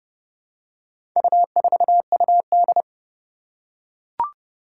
Morse code, heard as a keyed tone, about 30 wpm, 700 Hz. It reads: U4UD
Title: Four Challenging Letter-Number Combinations 30wpm